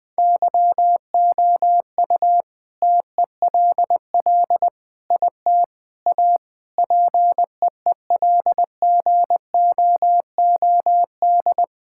Morse code, heard as a keyed tone, about 21 wpm, 700 Hz. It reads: YOU TELL IT A PEEL GOOD
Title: All-Age Appropriate Jokes - 20wpm